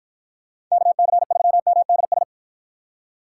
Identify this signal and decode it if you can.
Morse code, heard as a keyed tone, about 49 wpm, 700 Hz. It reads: KC3MDS